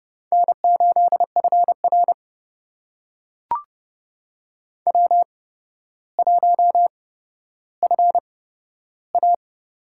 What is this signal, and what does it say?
Morse code, heard as a keyed tone, about 30 wpm, spaced as for 13 wpm, 700 Hz. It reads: N8FR W1FA
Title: USA Call Signs - 4 Characters - 8x Character Spacing - 30wpm